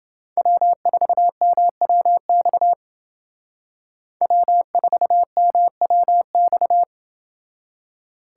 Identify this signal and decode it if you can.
Morse code, heard as a keyed tone, about 30 wpm, 700 Hz. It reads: W4MWX W4MWX